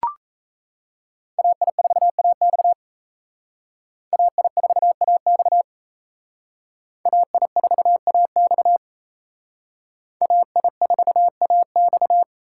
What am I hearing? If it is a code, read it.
Morse code, heard as a keyed tone, about 35 wpm, 700 Hz. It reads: AI4AX AI4AX AI4AX AI4AX
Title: Speed Racing - USA 2x2 Call Signs 28wpm